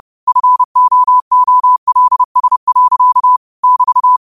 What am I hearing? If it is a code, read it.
Morse code, heard as a keyed tone, about 30 wpm, 1000 Hz. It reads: FOOLS. <BT>
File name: qotdfiles / 30wpm